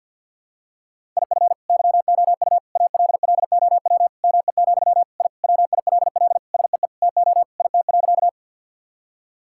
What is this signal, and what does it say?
Morse code, heard as a keyed tone, about 52 wpm, 700 Hz. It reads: IF YOU ALLOW ME, I WILL SEE TO IT.
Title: Sentences from Top 1000 Words - Review Part 5 - 50wpm